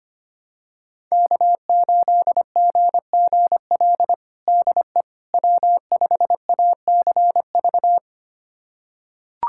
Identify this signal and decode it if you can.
Morse code, heard as a keyed tone, about 25 wpm, 700 Hz. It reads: K8GGL DE W5ACV
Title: USA Callsign DE Callsign - 5 Characters - 25wpm